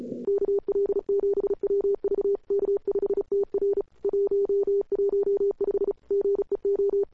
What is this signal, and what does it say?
Morse code, heard as a keyed tone, about 34 wpm, 410 Hz. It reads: KL7WV K5TR 115 GEO
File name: CWSprint / Feb2007 / 2007-feb-cw-sprint-k5tr-op-n5rz -01:00